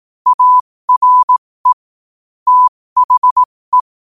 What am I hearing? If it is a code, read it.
Morse code, heard as a keyed tone, about 18 wpm, spaced as for 15 wpm, 1000 Hz. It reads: ARE THE
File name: qotdfiles / 15wpm